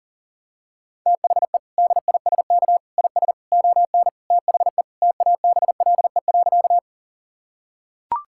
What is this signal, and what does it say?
Morse code, heard as a keyed tone, about 40 wpm, 700 Hz. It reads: THE DISK IS ON THE TABLE.
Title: Sentences from Top 1000 Words - Review Part 6 - 40wpm